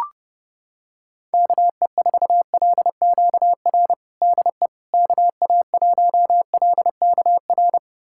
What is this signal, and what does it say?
Morse code, heard as a keyed tone, about 30 wpm, 700 Hz. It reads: KE4LQR DE KA1LKR
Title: USA Callsign DE Callsign - 6 Characters - 30wpm